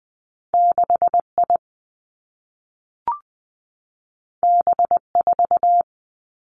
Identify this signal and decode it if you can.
Morse code, heard as a keyed tone, about 20 wpm, 700 Hz. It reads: 6 I B 4
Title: Two Challenging Letter-Number Combinations 20wpm